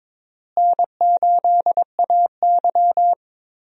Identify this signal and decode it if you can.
Morse code, heard as a keyed tone, about 22 wpm, 700 Hz. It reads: N8AY
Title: USA Call Signs - 4 Characters - Repeated 3x - 22wpm